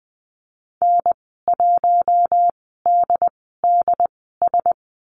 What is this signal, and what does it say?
Morse code, heard as a keyed tone, about 20 wpm, spaced as for 14 wpm, 700 Hz. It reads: N1DDS
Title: 076 Lesson 13g - Letter L - Call Signs 20wpm